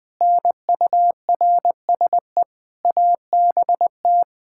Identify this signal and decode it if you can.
Morse code, heard as a keyed tone, about 20 wpm, 700 Hz. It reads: NURSE ABT
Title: K1USN SST QSO Practice 20wpm